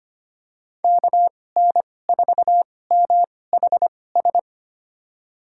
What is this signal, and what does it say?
Morse code, heard as a keyed tone, about 25 wpm, spaced as for 18 wpm, 700 Hz. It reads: KN4MHS
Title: USA Call Signs - 2x Character Spacing - 25wpm